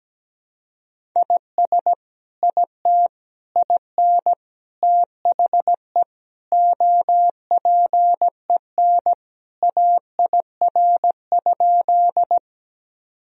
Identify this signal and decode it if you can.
Morse code, heard as a keyed tone, about 17 wpm, 700 Hz. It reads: IS IT IN THE OPEN AIR?
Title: Sentences from Top 1000 Words - Review Part 1 - 17wpm